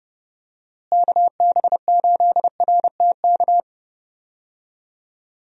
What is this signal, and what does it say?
Morse code, heard as a keyed tone, about 30 wpm, 700 Hz. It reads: KB8RTK
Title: USA Call Signs - 6 Characters - Repeated 2x - 30wpm